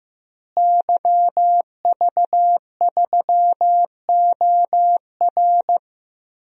Morse code, heard as a keyed tone, about 16 wpm, 700 Hz. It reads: YV3OR
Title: DXCC-100 Call Signs - 5 Characters - Repeated 3x - 15wpm